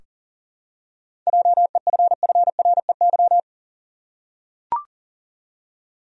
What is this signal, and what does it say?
Morse code, heard as a keyed tone, about 37 wpm, 700 Hz. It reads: JEFFREY
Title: USA Common Names - 40wpm